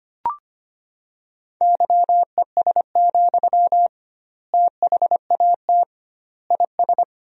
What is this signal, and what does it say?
Morse code, heard as a keyed tone, about 25 wpm, 700 Hz. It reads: YES, THAT IS
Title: 2x Word-Spacing - Sentences from Top 500 Words - 25wpm